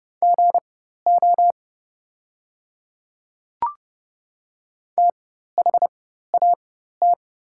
Morse code, heard as a keyed tone, about 30 wpm, spaced as for 14 wpm, 700 Hz. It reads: GO THAT
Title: 4x Character Spacing - Top 200 Words - 30wpm